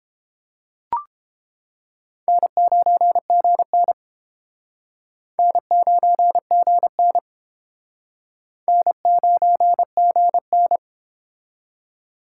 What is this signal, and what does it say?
Morse code, heard as a keyed tone, about 30 wpm, 700 Hz. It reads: N9GN N9GN N9GN